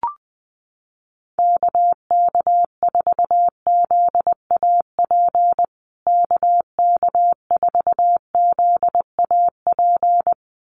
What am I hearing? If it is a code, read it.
Morse code, heard as a keyed tone, about 20 wpm, 700 Hz. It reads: KK4ZAP KK4ZAP
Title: USA Call Signs Repeated 2x 20wpm